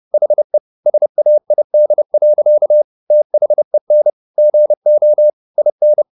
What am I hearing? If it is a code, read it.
Morse code, heard as a keyed tone, about 30 wpm, 600 Hz. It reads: HE SAID. THEN GO IN
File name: the old man and the sea / the old man and the sea 1 30wpm